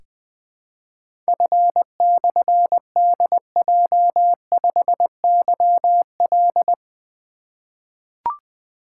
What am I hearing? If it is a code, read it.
Morse code, heard as a keyed tone, about 20 wpm, 700 Hz. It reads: F/DJ5YL
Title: Difficult Call Signs 20wpm